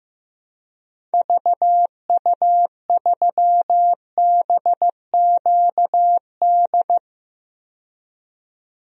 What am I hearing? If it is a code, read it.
Morse code, heard as a keyed tone, about 15 wpm, 700 Hz. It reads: VU3BQD
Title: DXCC-100 Call Signs - 6 Characters - 15wpm